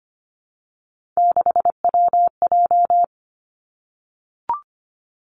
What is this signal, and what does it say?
Morse code, heard as a keyed tone, about 25 wpm, 700 Hz. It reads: 6WJ
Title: Three Challenging Letter-Number Combinations 25wpm